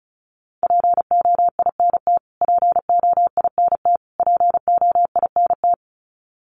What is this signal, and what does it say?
Morse code, heard as a keyed tone, about 34 wpm, 700 Hz. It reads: POINT POINT POINT
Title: Top 300 Words Repeated 35wpm